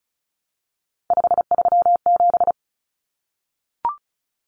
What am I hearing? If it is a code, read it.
Morse code, heard as a keyed tone, about 34 wpm, 700 Hz. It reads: 537
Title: Common Cut Numbers - 3-Digit 35wpm